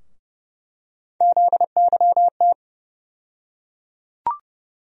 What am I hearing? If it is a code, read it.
Morse code, heard as a keyed tone, about 30 wpm, 700 Hz. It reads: ZYT
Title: USA Call Sign Postfixes - 3 Letters 30wpm